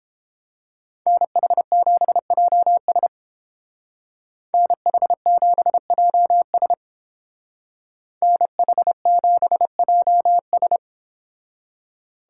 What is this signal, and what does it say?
Morse code, heard as a keyed tone, about 29 wpm, 700 Hz. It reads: NH7JS NH7JS NH7JS